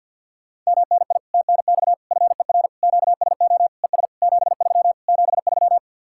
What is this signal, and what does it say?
Morse code, heard as a keyed tone, about 57 wpm, 700 Hz. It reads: MNI TNX FER QSO ES 73 73
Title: Speed-Racing Common QSO Phrases Encore - 50wpm